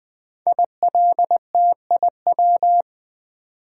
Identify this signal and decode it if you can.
Morse code, heard as a keyed tone, about 20 wpm, 700 Hz. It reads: ILTIW
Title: Code Groups - 5 Letters - 20wpm